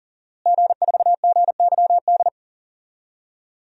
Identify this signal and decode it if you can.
Morse code, heard as a keyed tone, about 40 wpm, 700 Hz. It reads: G4GYD
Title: DXCC-100 Call Signs - 5 Characters - Repeated 2x - 40wpm